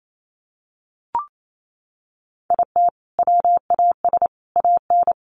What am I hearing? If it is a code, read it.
Morse code, heard as a keyed tone, about 28 wpm, 700 Hz. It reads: IT WAS AN
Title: Sentences from Top 1500 Words 28wpm